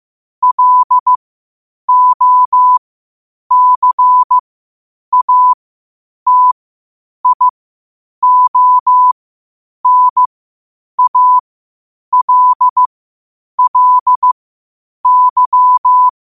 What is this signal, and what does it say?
Morse code, heard as a keyed tone, about 15 wpm, spaced as for 8 wpm, 1000 Hz. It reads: LOCATIONALLY
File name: qotdfiles / 8wpm